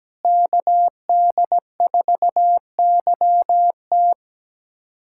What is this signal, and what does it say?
Morse code, heard as a keyed tone, about 17 wpm, 700 Hz. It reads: KD4YT
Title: USA Callsign DE Callsign - 5 Characters - 17wpm